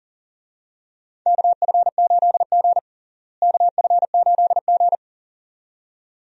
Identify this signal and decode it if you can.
Morse code, heard as a keyed tone, about 40 wpm, 700 Hz. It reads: KF8G KF8G
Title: USA Call Signs - 4 Characters - Repeated 2x - 40wpm